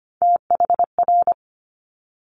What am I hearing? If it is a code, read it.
Morse code, heard as a keyed tone, about 25 wpm, 700 Hz. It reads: THR
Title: Multi-letter Phonemes Quick Fire 25wpm